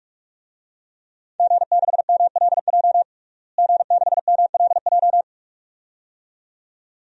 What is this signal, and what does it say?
Morse code, heard as a keyed tone, about 46 wpm, 700 Hz. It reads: G6MLJ G6MLJ
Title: DXCC-100 Call Signs - 5 Characters - Repeated 2x - 45wpm